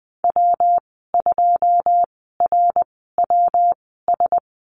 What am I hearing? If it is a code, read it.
Morse code, heard as a keyed tone, about 20 wpm, spaced as for 14 wpm, 700 Hz. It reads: W2RWS
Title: 161 Lesson 26g - Letter g - Call Signs 20wpm